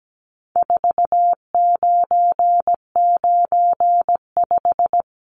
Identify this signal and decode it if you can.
Morse code, heard as a keyed tone, about 17 wpm, 700 Hz. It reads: 4995
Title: Numbers - 4-Digit 17wpm